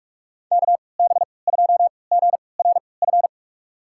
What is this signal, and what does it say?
Morse code, heard as a keyed tone, about 46 wpm, spaced as for 26 wpm, 700 Hz. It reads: KB2GRF
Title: USA Call Signs - 3x Character Spacing - 45wpm